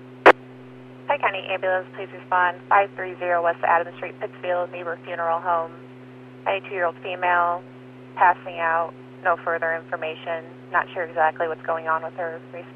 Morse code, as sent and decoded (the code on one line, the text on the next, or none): none